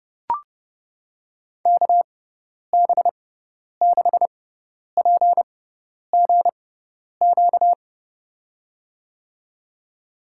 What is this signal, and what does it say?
Morse code, heard as a keyed tone, about 29 wpm, spaced as for 10 wpm, 700 Hz. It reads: KB6PGQ
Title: USA Call Signs - 6x Character Spacing - 30wpm